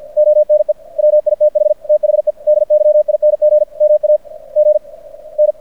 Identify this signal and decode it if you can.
Morse code, heard as a keyed tone, about 50 wpm, spaced as for 38 wpm, 600 Hz. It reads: ONE WITH THE DYING MAGN
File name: archived / CRSnet